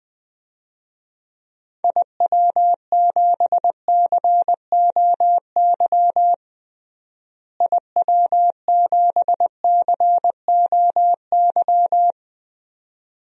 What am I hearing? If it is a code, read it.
Morse code, heard as a keyed tone, about 21 wpm, 700 Hz. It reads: IW7COY IW7COY
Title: DXCC-100 Call Signs - 6 Characters - Repeated 2x - 20wpm